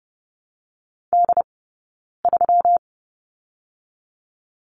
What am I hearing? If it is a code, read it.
Morse code, heard as a keyed tone, about 30 wpm, 700 Hz. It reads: D 3
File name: Lesson 20d - Number 3 - Two Characters 30wpm